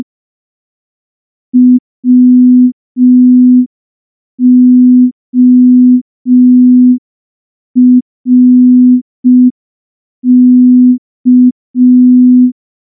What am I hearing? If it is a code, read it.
Morse code, heard as a keyed tone, about 5 wpm, 250 Hz. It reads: WORK
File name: morse / morse-training-05WPM